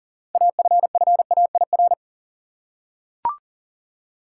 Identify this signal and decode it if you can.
Morse code, heard as a keyed tone, about 40 wpm, 700 Hz. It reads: AFFAIR
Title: Six Letter Words - 40wpm